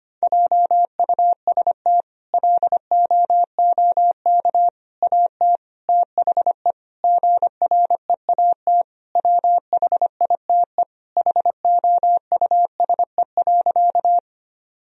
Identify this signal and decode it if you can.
Morse code, heard as a keyed tone, about 25 wpm, 700 Hz. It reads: JUST LOOK AT THE GREAT WHITE HOUSE.
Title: Sentences from Top 1000 Words - Review Part 1 - 25wpm